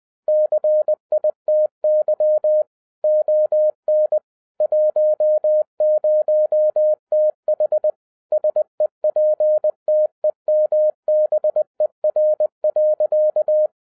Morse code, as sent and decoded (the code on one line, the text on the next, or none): -.-. .. - -.-- / --- -. / .---- ----- - .... / ... .--. - . -- -... .-. .-.-.-
CITY ON 10TH SEPTEMBER.